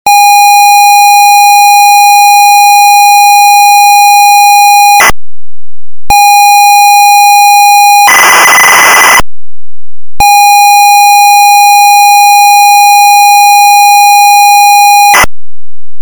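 ZX Spectrum tape loading.
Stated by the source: ZX Spectrum game tape loading.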